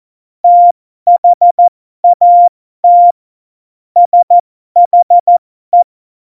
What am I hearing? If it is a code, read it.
Morse code, heard as a keyed tone, about 14 wpm, spaced as for 12 wpm, 700 Hz. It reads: THAT SHE
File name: Morsekurs / Witze